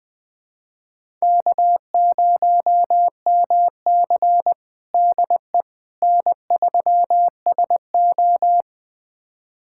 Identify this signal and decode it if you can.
Morse code, heard as a keyed tone, about 20 wpm, 700 Hz. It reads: K0MC DE N3SO
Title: USA Callsign DE Callsign - 4 Characters - 20wpm